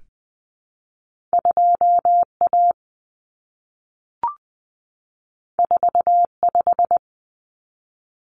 Morse code, heard as a keyed tone, about 20 wpm, 700 Hz. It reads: 2A 45
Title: Common Cut Numbers - 2-Digit 20wpm